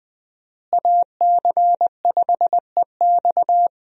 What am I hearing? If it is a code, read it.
Morse code, heard as a keyed tone, about 20 wpm, 700 Hz. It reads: AC5EX
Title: USA Call Signs - 5 Characters - Repeated 2x - 20wpm